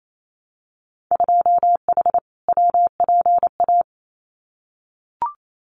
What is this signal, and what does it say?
Morse code, heard as a keyed tone, about 28 wpm, 700 Hz. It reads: 2H WPA
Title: Winter Field Day Exchanges 28wpm